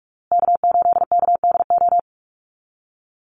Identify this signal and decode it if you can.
Morse code, heard as a keyed tone, about 44 wpm, 700 Hz. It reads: K8KDO
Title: USA Call Signs Repeated 2x 45wpm